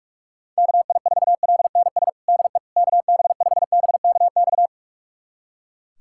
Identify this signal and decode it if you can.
Morse code, heard as a keyed tone, about 42 wpm, 700 Hz. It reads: KI4LNS DE KB5BKX
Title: USA Callsign DE Callsign 45wpm